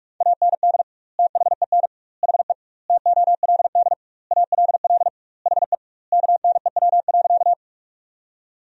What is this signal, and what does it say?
Morse code, heard as a keyed tone, about 46 wpm, 700 Hz. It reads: AND THEN HE TOLD ALL HE KNEW.
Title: 2x Word-Spacing - Sentences from Top 400 Words - 45wpm